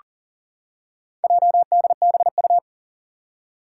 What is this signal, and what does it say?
Morse code, heard as a keyed tone, about 40 wpm, 700 Hz. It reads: JDBU
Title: Four Challenging Letter-Number Combinations 40wpm